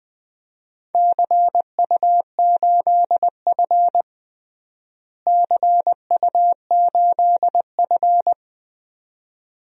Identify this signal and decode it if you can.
Morse code, heard as a keyed tone, about 20 wpm, 700 Hz. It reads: CU8F CU8F
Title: DXCC-100 Call Signs - 4 Characters - Repeated 2x - 20wpm